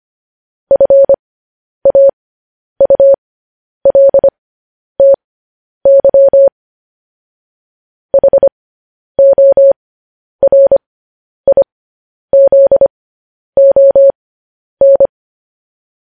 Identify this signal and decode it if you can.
Morse code, heard as a keyed tone, about 25 wpm, spaced as for 10 wpm, 550 Hz. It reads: FAULTY HORIZON
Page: news headlines in morse code at 10 wpm